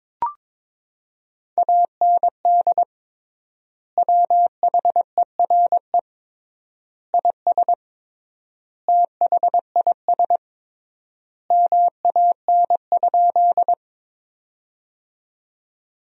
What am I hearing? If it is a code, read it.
Morse code, heard as a keyed tone, about 22 wpm, 700 Hz. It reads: AND WHERE IS THIS MAN?
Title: 3x Word-Spacing - Sentences from Top 200 Words - 22wpm